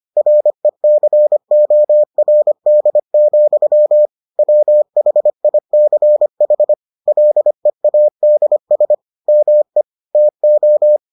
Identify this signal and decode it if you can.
Morse code, heard as a keyed tone, about 25 wpm, 600 Hz. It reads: RECORD, WHICH LEADS ME TO